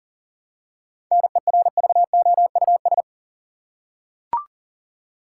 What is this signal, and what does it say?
Morse code, heard as a keyed tone, about 37 wpm, 700 Hz. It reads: NERVOUS
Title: Seven Letter Words - 40wpm